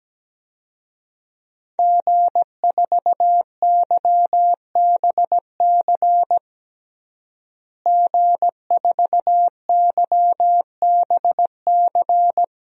Morse code, heard as a keyed tone, about 17 wpm, 700 Hz. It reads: G4YBC G4YBC
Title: DXCC-100 Call Signs - 5 Characters - Repeated 2x - 17wpm